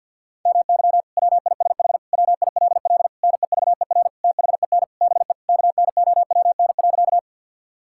Morse code, heard as a keyed tone, about 50 wpm, 700 Hz. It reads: MY WISH WILL NEVER THEN BE KNOWN.
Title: Sentences from Top 1600 Words 50wpm